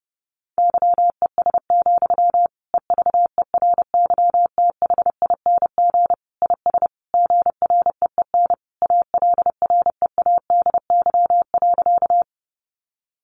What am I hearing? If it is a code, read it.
Morse code, heard as a keyed tone, about 30 wpm, 700 Hz. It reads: YES, EVERYTHING IS GREEN ALREADY.
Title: Sentences from Top 700 Words 30wpm